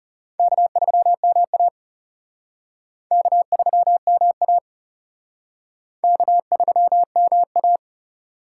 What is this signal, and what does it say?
Morse code, heard as a keyed tone, about 34 wpm, 700 Hz. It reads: K3MA K3MA K3MA
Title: Speed Racing - USA Call Signs 30wpm